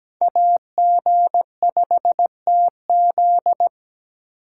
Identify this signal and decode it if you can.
Morse code, heard as a keyed tone, about 20 wpm, 700 Hz. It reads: AG5TZ
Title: Speed Racing - USA 2x2 Call Signs 17wpm